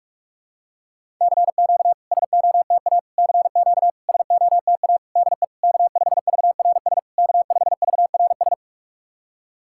Morse code, heard as a keyed tone, about 48 wpm, 700 Hz. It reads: CQ SOTA CQ SOTA DE K5VRS K5VRS
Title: SOTA QSO Practice 45wpm